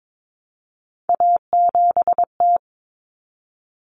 Morse code, heard as a keyed tone, about 22 wpm, 700 Hz. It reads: A7T
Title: Common Cut Numbers - 3-Digit 22wpm